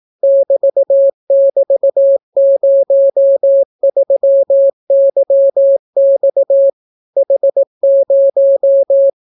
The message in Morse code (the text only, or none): <BT><BT>03YX H0